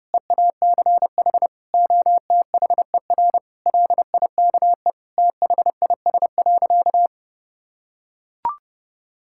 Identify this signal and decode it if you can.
Morse code, heard as a keyed tone, about 32 wpm, 700 Hz. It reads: EACH OTHER LIKE THIS.
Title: Sentences from Top 1000 Words - Review Part 6 - 30wpm